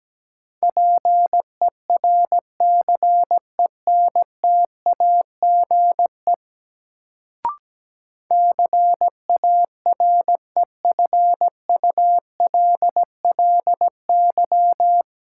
Morse code, heard as a keyed tone, about 17 wpm, 700 Hz. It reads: PERCENTAGE CAREFULLY
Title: Sentences from Top 1300 Words - New Word Drill Alt - 17wpm